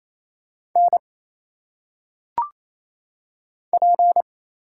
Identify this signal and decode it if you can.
Morse code, heard as a keyed tone, about 27 wpm, 700 Hz. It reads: N P
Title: Single Letters - Phonetic Pronunciation 28wpm